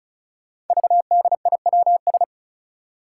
Morse code, heard as a keyed tone, about 34 wpm, 700 Hz. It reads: VDIWS
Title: Five Challenging Letter-Number Combinations 35wpm